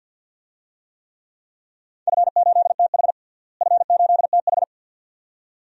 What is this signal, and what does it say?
Morse code, heard as a keyed tone, about 52 wpm, 700 Hz. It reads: F8TH F8TH
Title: DXCC-100 Call Signs - 4 Characters - Repeated 2x - 50wpm